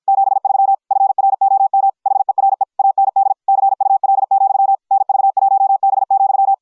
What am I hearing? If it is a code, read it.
Morse code, heard as a keyed tone, about 53 wpm, 750 Hz. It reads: 73 FROM HERE AND CUL, NV1B,